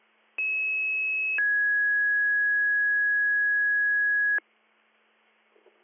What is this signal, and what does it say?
Two-tone page, heard as a one-second tone, then a three-second tone, about 2600 then 1650 Hz. It is heard from 0.4 to 4.4 s.